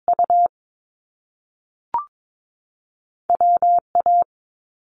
Morse code, heard as a keyed tone, about 22 wpm, 700 Hz. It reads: U W A
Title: Top Two Letter Combinations in Words 22wpm